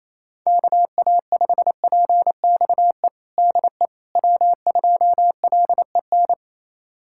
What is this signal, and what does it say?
Morse code, heard as a keyed tone, about 28 wpm, 700 Hz. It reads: KA5PXE DE W2LEN
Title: USA Callsign DE Callsign 28wpm